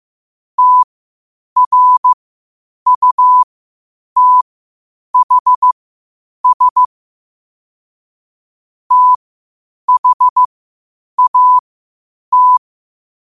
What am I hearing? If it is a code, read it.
Morse code, heard as a keyed tone, about 15 wpm, spaced as for 8 wpm, 1000 Hz. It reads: TRUTHS THAT
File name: qotdfiles / 8wpm